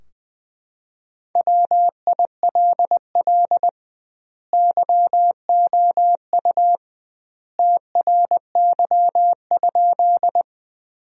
Morse code, heard as a keyed tone, about 20 wpm, 700 Hz. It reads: WILL YOU TRY?
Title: 2x Word-Spacing - Sentences from Top 200 Words - 20wpm